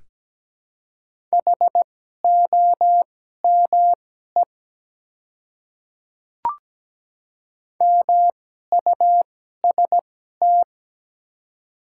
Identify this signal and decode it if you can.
Morse code, heard as a keyed tone, about 17 wpm, spaced as for 12 wpm, 700 Hz. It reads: HOME MUST
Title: 2x Character Spacing - Top 300 Words - 17wpm